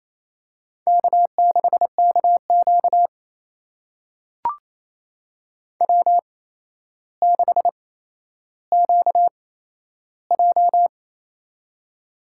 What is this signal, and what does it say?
Morse code, heard as a keyed tone, about 27 wpm, spaced as for 12 wpm, 700 Hz. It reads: K6KQ W6QJ
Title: USA Call Signs - 4 Characters - 8x Character Spacing - 28wpm